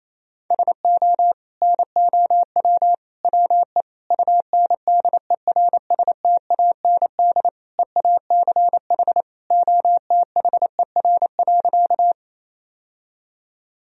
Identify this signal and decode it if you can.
Morse code, heard as a keyed tone, about 28 wpm, 700 Hz. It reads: SO NOW WE UNDERSTAND EACH OTHER.
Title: Sentences from Top 1000 Words - Review Part 1 - 28wpm